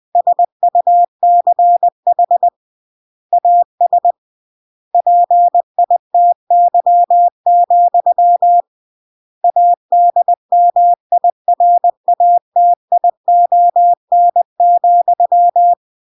Morse code, heard as a keyed tone, about 20 wpm, 700 Hz. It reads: SUCH AS PITY, ADMIRATION,
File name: CW / jodorowsky / 82 commandments 20-20